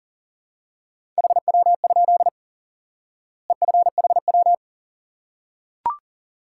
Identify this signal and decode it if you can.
Morse code, heard as a keyed tone, about 40 wpm, 700 Hz. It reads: HW? EFHW
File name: IQE Course 3 - Lesson 89 - YAGI - Two Elements - 40wpm